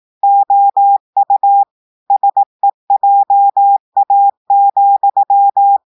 Morse code, heard as a keyed tone, about 18 wpm, 800 Hz. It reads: OU SEJA,
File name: audios / portugais / portugais texte en clair 2 vitesse 18 mots minutes